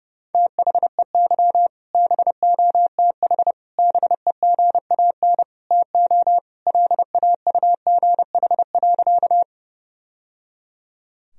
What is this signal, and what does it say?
Morse code, heard as a keyed tone, about 32 wpm, 700 Hz. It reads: THEY BOTH BEGAN TO LAUGH.
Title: Sentences from Top 1000 Words - Review Part 1 - 30wpm